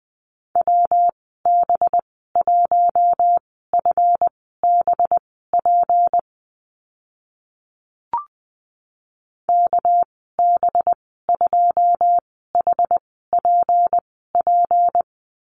Morse code, heard as a keyed tone, about 20 wpm, spaced as for 14 wpm, 700 Hz. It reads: WB1FBP KB2HPP
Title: 213 Lesson 34g - Number 8 - Call Signs 20wpm